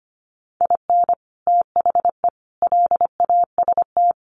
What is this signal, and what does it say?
Morse code, heard as a keyed tone, about 25 wpm, 700 Hz. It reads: IN THE LAST
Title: Sets of 5 Words 25wpm